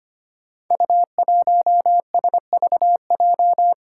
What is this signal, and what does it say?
Morse code, heard as a keyed tone, about 25 wpm, 700 Hz. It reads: U1SVJ
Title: Five Challenging Letter-Number Combinations 25wpm